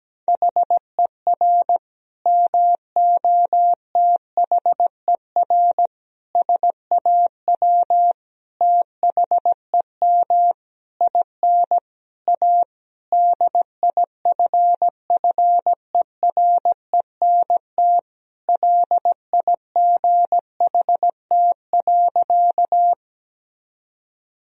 Morse code, heard as a keyed tone, about 17 wpm, 700 Hz. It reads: HER MOTHER SAW THEM IN A DIFFERENT LIGHT.
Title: Sentences from Top 1000 Words - Review Part 3 - 17wpm